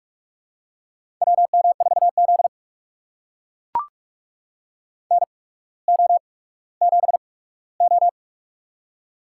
Morse code, heard as a keyed tone, about 44 wpm, spaced as for 19 wpm, 690 Hz. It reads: WM4Z NO7O
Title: USA Call Signs - 4 Characters - 8x Character Spacing - 45wpm